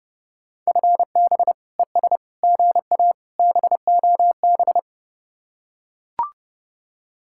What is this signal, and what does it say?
Morse code, heard as a keyed tone, about 30 wpm, 700 Hz. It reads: FB ES GA BOB